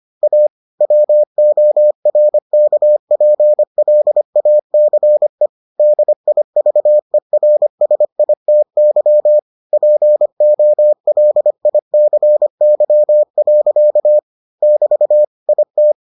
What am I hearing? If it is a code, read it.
Morse code, heard as a keyed tone, about 25 wpm, 600 Hz. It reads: A WORKPLACE DIVERSITY POLICY. <BT> IT